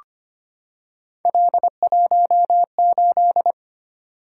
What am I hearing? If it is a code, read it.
Morse code, heard as a keyed tone, about 25 wpm, 700 Hz. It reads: L18
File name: Three Letter-Number Combinations - 25wpm